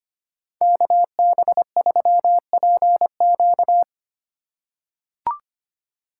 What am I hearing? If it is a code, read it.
Morse code, heard as a keyed tone, about 25 wpm, 700 Hz. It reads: KB3PQ
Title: USA Call Signs - 5 Characters - 2x Character Spacing - 25wpm